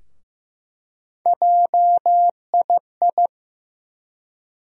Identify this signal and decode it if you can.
Morse code, heard as a keyed tone, about 15 wpm, 700 Hz. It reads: JII
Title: USA Call Sign Postfixes - 3 Letters 15wpm